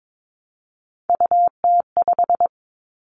Morse code, heard as a keyed tone, about 22 wpm, 700 Hz. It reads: UT5
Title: Callsign Prefixes for Top 25 DXCC - 22wpm